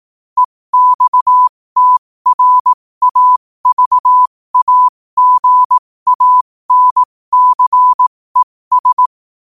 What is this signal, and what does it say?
Morse code, heard as a keyed tone, about 18 wpm, spaced as for 15 wpm, 1000 Hz. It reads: EXTRAVAGANCES
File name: qotdfiles / 15wpm